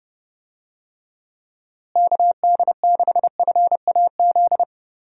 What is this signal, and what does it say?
Morse code, heard as a keyed tone, about 30 wpm, 700 Hz. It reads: KD6FAZ
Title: USA Call Signs - 6 Characters - Repeated 2x - 30wpm